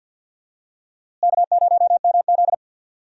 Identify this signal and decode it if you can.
Morse code, heard as a keyed tone, about 48 wpm, 700 Hz. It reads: K0MZ
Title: USA Call Signs - 4 Characters - Repeated 3x - 50wpm